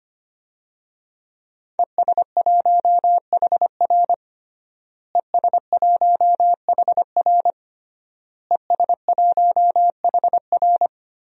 Morse code, heard as a keyed tone, about 25 wpm, 700 Hz. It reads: ES1HR ES1HR ES1HR
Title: DXCC-100 Call Signs - 5 Characters - Repeated 3x - 25wpm